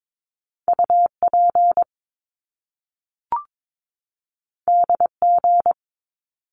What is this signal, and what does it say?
Morse code, heard as a keyed tone, about 22 wpm, 700 Hz. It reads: UP DG
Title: Top Two Letter Combinations in Words 22wpm